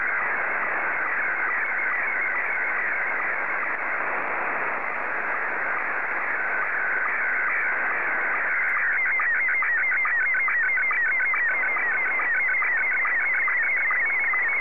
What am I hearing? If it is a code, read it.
SSTV (slow-scan television), Scottie 1.